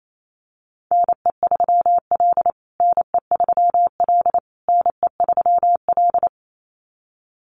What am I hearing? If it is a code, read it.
Morse code, heard as a keyed tone, about 28 wpm, 700 Hz. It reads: NE3L NE3L NE3L
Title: USA Call Signs Repeated 3x 28wpm